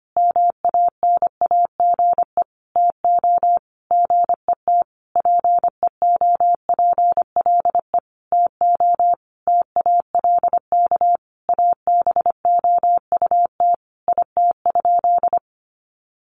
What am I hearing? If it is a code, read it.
Morse code, heard as a keyed tone, about 25 wpm, 700 Hz. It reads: MANAGE TO GET PEOPLE TO TALK ABOUT IT?
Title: Sentences from Top 1400 Words 25wpm